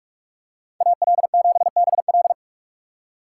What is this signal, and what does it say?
Morse code, heard as a keyed tone, about 43 wpm, 700 Hz. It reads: AL7BL
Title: USA Call Signs - 5 Characters - 4x Character Spacing - 45wpm